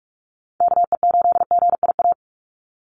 Morse code, heard as a keyed tone, about 46 wpm, 690 Hz. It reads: KE8GIA
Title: USA Call Signs 45wpm